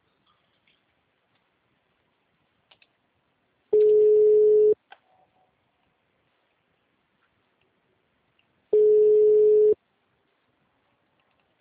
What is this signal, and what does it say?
Ringback tone (Europe, 425 Hz).
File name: out- -2043- - - .1788